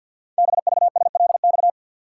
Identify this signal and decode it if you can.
Morse code, heard as a keyed tone, about 50 wpm, 700 Hz. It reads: BVSLX
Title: Code Groups - 5 Letter:Numbers - 50wpm